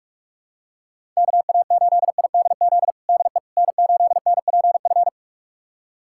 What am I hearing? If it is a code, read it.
Morse code, heard as a keyed tone, about 46 wpm, 700 Hz. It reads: KA8IDZ DE N8NPF
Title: USA Callsign DE Callsign 45wpm